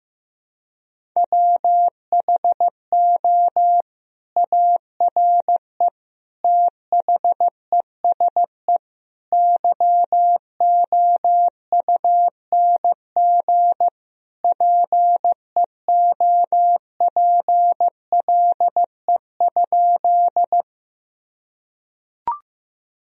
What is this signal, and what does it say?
Morse code, heard as a keyed tone, about 15 wpm, 700 Hz. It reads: WHO ARE THESE YOUNG PEOPLE?
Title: Sentences from Top 1000 Words - Review Part 1 - 15wpm